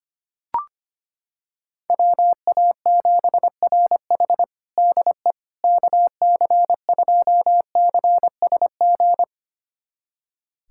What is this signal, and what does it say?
Morse code, heard as a keyed tone, about 25 wpm, 700 Hz. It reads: WA7RH DE KC2CSG
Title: USA Callsign DE Callsign 25wpm